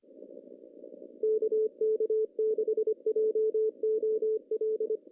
Morse code, heard as a keyed tone, about 25 wpm, 440 Hz. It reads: KK6JOL
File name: CW / cqww-cw-k5tr